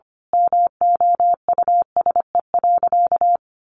Morse code, heard as a keyed tone, about 25 wpm, 700 Hz. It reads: MOUSE.
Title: A Sound Of Thunder 25wpm